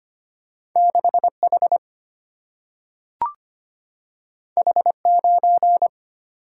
Morse code, heard as a keyed tone, about 25 wpm, 700 Hz. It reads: 6 H H 9
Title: Single-Letter Single-Number 25wpm